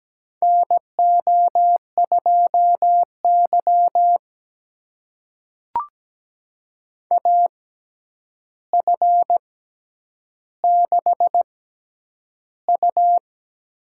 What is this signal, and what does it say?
Morse code, heard as a keyed tone, about 17 wpm, spaced as for 9 wpm, 700 Hz. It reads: NO2Y AF6U
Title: USA Call Signs - 4 Characters - 6x Character Spacing - 17wpm